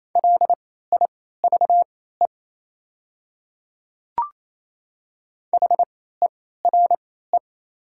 Morse code, heard as a keyed tone, about 28 wpm, spaced as for 16 wpm, 700 Hz. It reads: LIVE HERE
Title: 3x Character Spacing - Top 200 Words - 28wpm